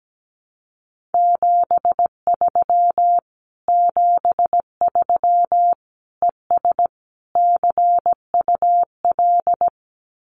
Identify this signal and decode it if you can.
Morse code, heard as a keyed tone, about 17 wpm, 700 Hz. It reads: 73 73 ES CUL
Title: Common QSO Phrases Encore 17wpm